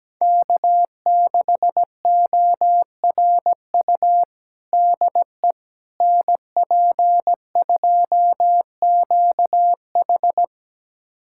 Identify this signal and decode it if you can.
Morse code, heard as a keyed tone, about 17 wpm, 700 Hz. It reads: K6ORU DE NP2QH
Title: USA Callsign DE Callsign - 5 Characters - 17wpm